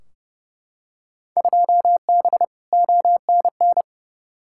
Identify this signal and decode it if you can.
Morse code, heard as a keyed tone, about 30 wpm, 700 Hz. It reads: 2B ONN
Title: Field Day Exchanges Encore - 30wpm